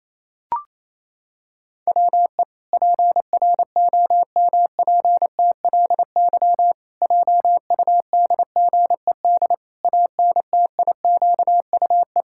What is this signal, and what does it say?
Morse code, heard as a keyed tone, about 28 wpm, 700 Hz. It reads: WE PROMPTLY JUDGED ANTIQUE
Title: Pangram Challenge v2 - 28wpm